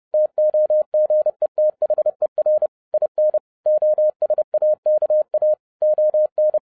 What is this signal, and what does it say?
Morse code, heard as a keyed tone, about 30 wpm, 610 Hz. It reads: TOGETHER IN OSAKA ON